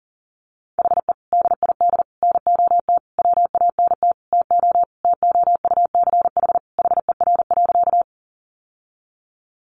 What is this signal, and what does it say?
Morse code, heard as a keyed tone, about 40 wpm, 700 Hz. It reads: HE DID NOT WANT TO TOUCH HER.
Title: Sentences from Top 1200 Words 40wpm